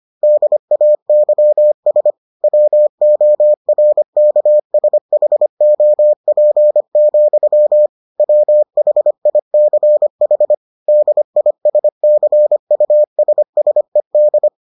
Morse code, heard as a keyed tone, about 25 wpm, 600 Hz. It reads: DAYS WORKSHOP, WHICH DISCUSSED